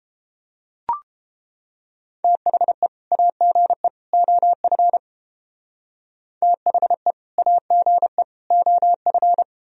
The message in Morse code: THE AGE OF THE AGE OF

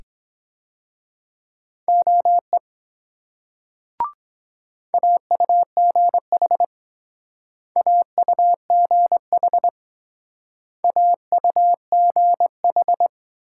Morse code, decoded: OE AUGH AUGH AUGH

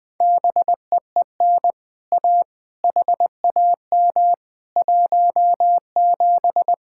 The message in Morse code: BEEN A HAM 17

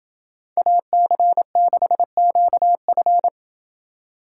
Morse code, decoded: AC6QF

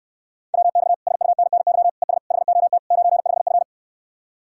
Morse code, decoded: WX SUNNY ES HOT 95F

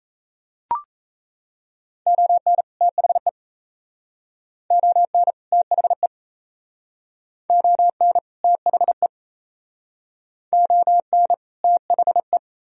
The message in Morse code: ON THE ON THE ON THE ON THE

